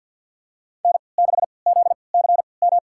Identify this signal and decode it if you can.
Morse code, heard as a keyed tone, about 52 wpm, spaced as for 29 wpm, 700 Hz. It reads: N6ZCM
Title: USA Call Signs - 5 Characters - 3x Character Spacing - 50wpm